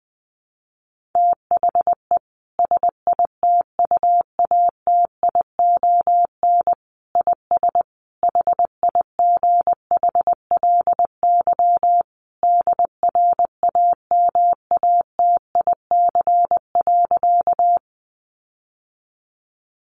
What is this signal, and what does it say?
Morse code, heard as a keyed tone, about 20 wpm, 700 Hz. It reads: THE SITUATION IS HIGHLY DRAMATIC.
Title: Sentences from Top 1300 Words 20wpm